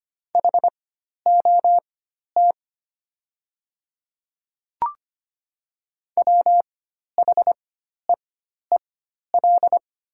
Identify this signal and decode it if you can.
Morse code, heard as a keyed tone, about 25 wpm, spaced as for 12 wpm, 690 Hz. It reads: HOT WHEEL
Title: 4x Character Spacing - Top 500 Words - 25wpm